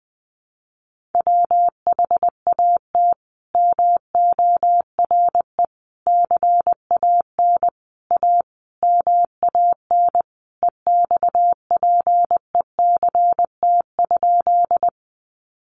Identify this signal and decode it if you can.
Morse code, heard as a keyed tone, about 20 wpm, 700 Hz. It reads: WHAT MORE CAN A MAN EXPECT?
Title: Sentences from Top 1100 Words 20wpm